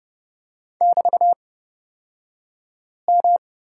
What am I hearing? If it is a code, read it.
Morse code, heard as a keyed tone, about 31 wpm, spaced as for 5 wpm, 700 Hz. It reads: <BT>M
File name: Lesson 36d - BT - Two Characters 30wpm